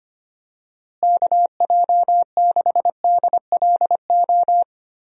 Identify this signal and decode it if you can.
Morse code, heard as a keyed tone, about 25 wpm, 700 Hz. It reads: KJ6DLO
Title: USA Call Signs - 6 Characters - Repeated 3x - 25wpm